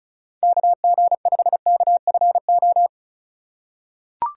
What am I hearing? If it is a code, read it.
Morse code, heard as a keyed tone, about 36 wpm, 700 Hz. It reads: KG5KFO
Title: USA Call Signs - 6 Characters - 4x Character Spacing - 35wpm